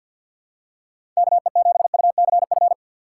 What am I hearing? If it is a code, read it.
Morse code, heard as a keyed tone, about 53 wpm, 700 Hz. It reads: KE7UCF
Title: USA Call Signs - 3x Character Spacing - 50wpm